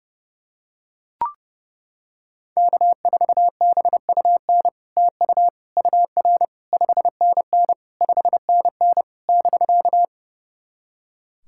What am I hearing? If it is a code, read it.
Morse code, heard as a keyed tone, about 30 wpm, 700 Hz. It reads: K4BUN TU UR 5NN 5NN <BK>
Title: SOTA QSO Practice 30wpm